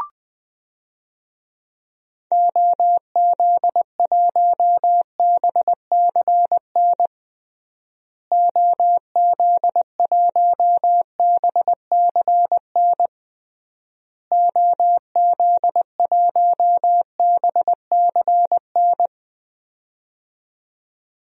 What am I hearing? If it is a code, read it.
Morse code, heard as a keyed tone, about 20 wpm, 700 Hz. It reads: OZ1BCN OZ1BCN OZ1BCN
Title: DXCC-100 Call Signs - 6 Characters - Repeated 3x - 20wpm